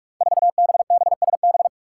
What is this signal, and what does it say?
Morse code, heard as a keyed tone, about 44 wpm, 700 Hz. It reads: 4BBSB
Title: Five Challenging Letter-Number Combinations 45wpm